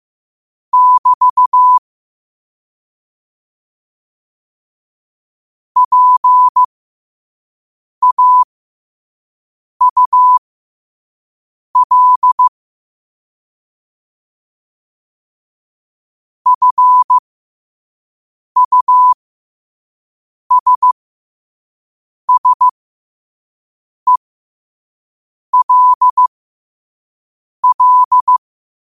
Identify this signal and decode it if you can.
Morse code, heard as a keyed tone, about 15 wpm, spaced as for 5 wpm, 1000 Hz. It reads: <BT> PAUL FUSSELL
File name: qotdfiles / 5wpm